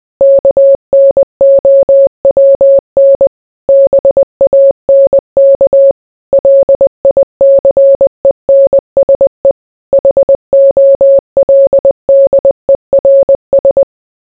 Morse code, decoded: KNOWN BANK LICENSE HOLDERS

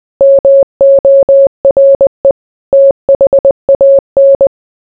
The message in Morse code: MORE THAN

